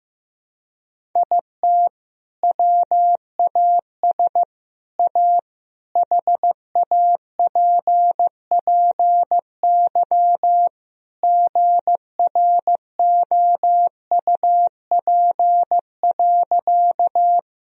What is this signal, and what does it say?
Morse code, heard as a keyed tone, about 15 wpm, 700 Hz. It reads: IT WAS A HAPPY GROUP.
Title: Sentences from Top 1000 Words - Review Part 1 - 15wpm